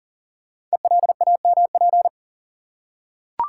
Morse code, heard as a keyed tone, about 40 wpm, 700 Hz. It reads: ELAMP
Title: Code Groups - 5 Letters - 40wpm